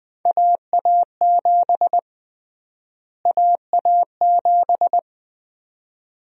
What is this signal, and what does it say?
Morse code, heard as a keyed tone, about 20 wpm, 700 Hz. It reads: AA7 AA7
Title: USA Call Sign Prefixes - Repeated 2x - 20wpm